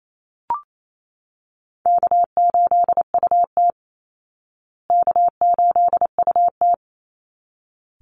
Morse code, heard as a keyed tone, about 28 wpm, 700 Hz. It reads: K8UT K8UT
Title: Easy Call Signs Repeated 2x - 28wpm